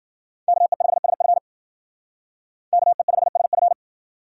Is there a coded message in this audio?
Yes: Morse code from 0.5 to 3.7 s, heard as a keyed tone, about 51 wpm, 700 Hz. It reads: KE5SF KE5SF